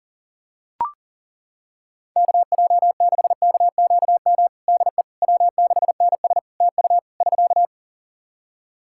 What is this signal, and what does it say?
Morse code, heard as a keyed tone, about 40 wpm, 700 Hz. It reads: KJ6KQM DE W6NS TU <SK>
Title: Practice QSO - Exchange 4 - Ending - 40wpm